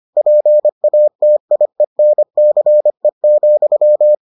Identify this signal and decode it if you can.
Morse code, heard as a keyed tone, about 25 wpm, 600 Hz. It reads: PATIENCE,